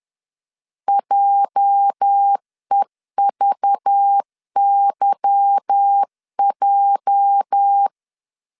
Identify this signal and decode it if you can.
Morse code, heard as a keyed tone, about 10 wpm, 800 Hz. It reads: JEVYJ
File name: LEZIONI / 8 YJ45 / 8 YJ 10wpm 2651